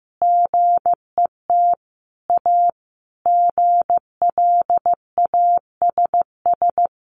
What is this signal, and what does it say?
Morse code, heard as a keyed tone, about 15 wpm, 700 Hz. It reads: GET A GLASS
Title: Sentences from Top 1300 Words 15wpm